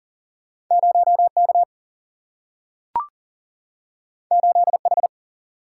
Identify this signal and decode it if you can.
Morse code, heard as a keyed tone, about 40 wpm, 700 Hz. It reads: 0 K 8 H
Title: Single-Letter Single-Number 40wpm